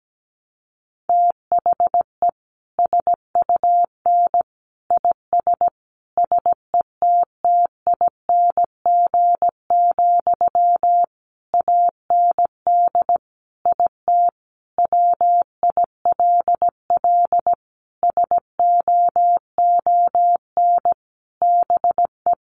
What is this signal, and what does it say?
Morse code, heard as a keyed tone, about 17 wpm, 700 Hz. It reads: THE SUN IS SETTING, AND IT WILL SOON BE
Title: Sentences from Top 1300 Words 17wpm